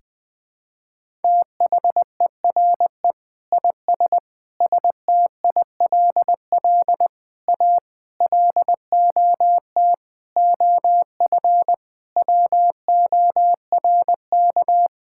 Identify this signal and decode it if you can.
Morse code, heard as a keyed tone, about 20 wpm, 700 Hz. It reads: THERE IS STILL A LOT OF WORK
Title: Sentences with 9-Word Phrases - 20wpm